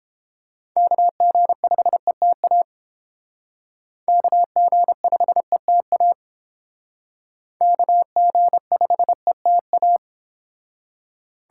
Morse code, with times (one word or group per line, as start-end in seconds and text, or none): 0.8-2.6 KG5ETA
4.1-6.1 KG5ETA
7.6-10.0 KG5ETA